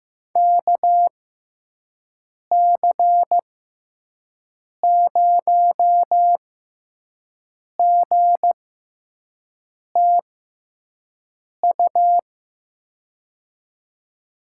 Morse code, heard as a keyed tone, about 15 wpm, spaced as for 5 wpm, 700 Hz. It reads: KC0GTU
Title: USA Call Signs - 6x Character Spacing - 15wpm